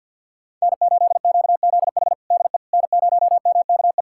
Morse code, heard as a keyed tone, about 47 wpm, 690 Hz. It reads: N8QZH DE N0MKE
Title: USA Callsign DE Callsign - 5 Characters - 50wpm